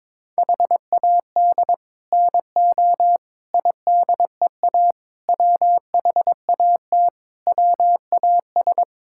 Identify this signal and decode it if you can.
Morse code, heard as a keyed tone, about 22 wpm, 700 Hz. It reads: HAD NO IDEA WHAT WAS
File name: practice / 22wpm / Sets of 7 Words - 22wpm